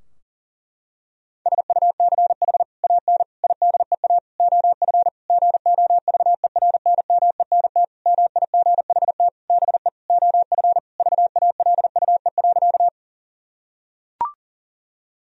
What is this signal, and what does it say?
Morse code, heard as a keyed tone, about 40 wpm, 700 Hz. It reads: SUCH AN IDEA OF GOVERNMENT MIGHT BE OF VALUE.
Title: Sentences from Top 1000 Words - Review Part 3 - 40wpm